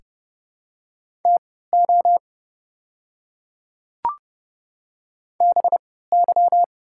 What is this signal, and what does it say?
Morse code, heard as a keyed tone, about 31 wpm, spaced as for 17 wpm, 700 Hz. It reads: TO BY
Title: 3x Character Spacing - Top 100 Words - 30wpm